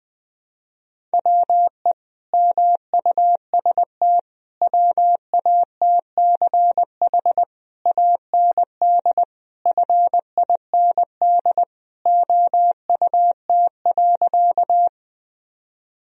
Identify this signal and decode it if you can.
Morse code, heard as a keyed tone, about 19 wpm, 700 Hz. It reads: WE MUST WATCH AND FIND OUT.
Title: Sentences from Top 1000 Words - Review Part 1 - 20wpm